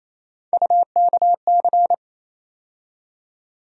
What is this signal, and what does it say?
Morse code, heard as a keyed tone, about 28 wpm, 700 Hz. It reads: UKC
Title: USA Call Sign Postfixes - 3 Letters 28wpm